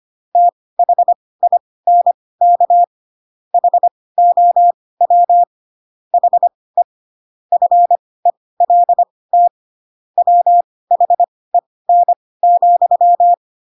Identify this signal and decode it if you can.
Morse code, heard as a keyed tone, about 25 wpm, spaced as for 18 wpm, 700 Hz. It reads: THINK HOW HE FELT WHEN,
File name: Pinocchio ch3 12 25 ik1hge